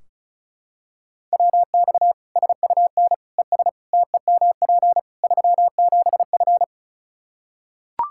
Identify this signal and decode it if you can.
Morse code, heard as a keyed tone, about 34 wpm, 700 Hz. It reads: WX SUN ES TEMP 37F
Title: K1USN SST QSO Practice 35wpm